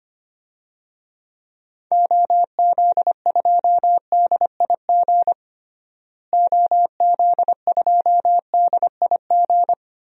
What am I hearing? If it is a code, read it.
Morse code, heard as a keyed tone, about 25 wpm, 700 Hz. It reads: OZ2DIG OZ2DIG
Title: DXCC-100 Call Signs - 6 Characters - Repeated 2x - 25wpm